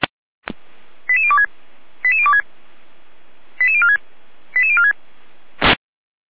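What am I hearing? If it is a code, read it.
Five-tone selective call (ZVEI), heard as quick runs of five tones, tones of about 70 ms. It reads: 88026 88026 88046 88046
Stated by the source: ZVEI five-tone call received on SDR.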